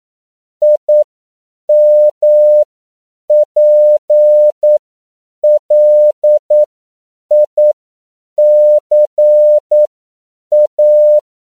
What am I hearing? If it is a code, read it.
Morse code, heard as a keyed tone, about 9 wpm, spaced as for 7 wpm, 600 Hz. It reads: IMPLICA
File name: CW / 10PPM